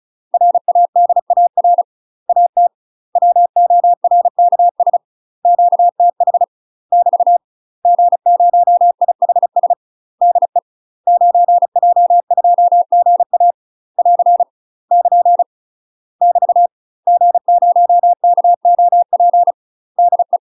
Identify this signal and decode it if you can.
Morse code, heard as a keyed tone, about 34 wpm, 700 Hz. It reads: RADAR AT WORKS QTH <BT> G0IHS DE 9J2GA <AR> <KN> <BT> G0KOP DE
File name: UK MASTER QSOs TRAINING/35WPM